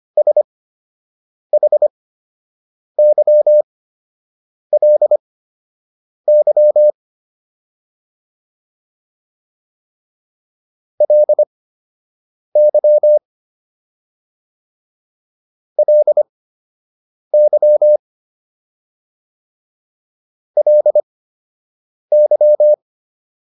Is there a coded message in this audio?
Yes: Morse code from 0.2 to 22.7 s, heard as a keyed tone, about 25 wpm, spaced as for 7 wpm, 600 Hz. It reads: SHYLY LY LY LY